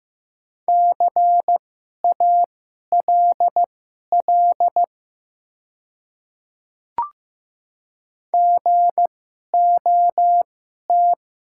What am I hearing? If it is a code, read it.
Morse code, heard as a keyed tone, about 15 wpm, spaced as for 11 wpm, 700 Hz. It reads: CALL GOT
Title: 2x Character Spacing - Top 300 Words - 15wpm